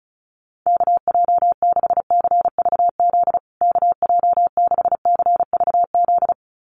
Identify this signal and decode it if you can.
Morse code, heard as a keyed tone, about 34 wpm, 700 Hz. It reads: KJ6CVZ KJ6CVZ
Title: USA Call Signs Repeated 2x 35wpm